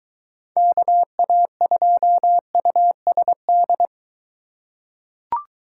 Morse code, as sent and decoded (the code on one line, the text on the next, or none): -.- .- ..--- ..- ... -..
KA2USD